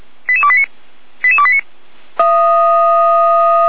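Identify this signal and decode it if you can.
Five-tone selective call (ZVEI), heard as quick runs of five tones, tones of about 70 ms. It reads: 88289 88289
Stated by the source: ZVEI five-tone selective call received on an SDR.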